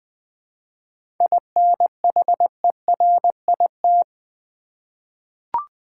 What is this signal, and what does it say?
Morse code, heard as a keyed tone, about 20 wpm, 700 Hz. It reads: INHERIT
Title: 7 Letter Words - First 3 Letters Given - 20wpm